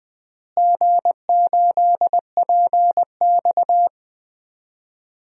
Morse code, heard as a keyed tone, about 20 wpm, 700 Hz. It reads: G8PX
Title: DXCC-100 Call Signs - 4 Characters - Repeated 3x - 20wpm